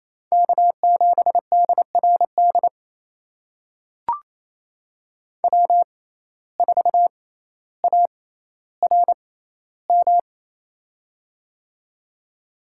Morse code, heard as a keyed tone, about 28 wpm, spaced as for 14 wpm, 700 Hz. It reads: K7DRD W4ARM
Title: USA Call Signs - 5 Characters - 6x Character Spacing - 28wpm